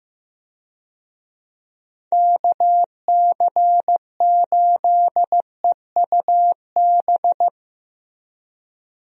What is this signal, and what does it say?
Morse code, heard as a keyed tone, about 15 wpm, 700 Hz. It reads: KC8EUB
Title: USA Call Signs - 6 Characters - Repeated 2x - 15wpm